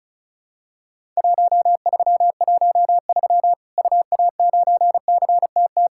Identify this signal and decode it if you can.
Morse code, heard as a keyed tone, about 36 wpm, 700 Hz. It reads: 1313 UA9CTT
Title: Worked All Europe DX Contest - QTC Exchange - 35wpm